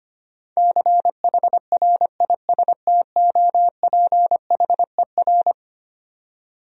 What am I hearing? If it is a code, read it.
Morse code, heard as a keyed tone, about 25 wpm, 700 Hz. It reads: CHRISTOPHER
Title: USA Common Names - 25wpm